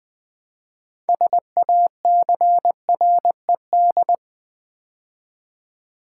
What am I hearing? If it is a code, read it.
Morse code, heard as a keyed tone, about 20 wpm, 700 Hz. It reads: SACRED
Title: Build A Word - 2 Letters At A Time - 6-Letter Words - 20wpm